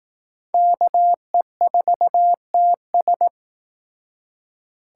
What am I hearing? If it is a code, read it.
Morse code, heard as a keyed tone, about 18 wpm, 700 Hz. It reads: KE4TS
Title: Speed Racing - USA 2x2 Call Signs 15wpm